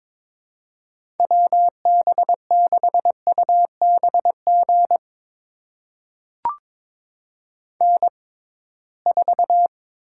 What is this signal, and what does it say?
Morse code, heard as a keyed tone, about 22 wpm, spaced as for 17 wpm, 700 Hz. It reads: WB6UBG N4
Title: USA Call Signs - 6x Character Spacing - 22wpm